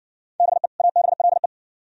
Morse code, heard as a keyed tone, about 57 wpm, 690 Hz. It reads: BE ABLE